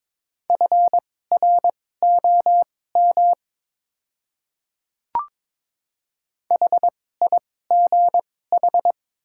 Morse code, heard as a keyed tone, about 22 wpm, spaced as for 16 wpm, 700 Hz. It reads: FROM HIGH
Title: 2x Character Spacing - Top 200 Words - 22wpm